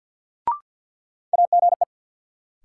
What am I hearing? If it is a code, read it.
Morse code, heard as a keyed tone, about 46 wpm, 700 Hz. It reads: AGE
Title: Sentences from Top 500 Words - New Word Drill - 50wpm